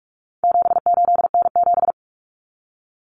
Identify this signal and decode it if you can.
Morse code, heard as a keyed tone, about 44 wpm, 700 Hz. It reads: 78N7
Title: Common Cut Numbers - 4-Digit 45wpm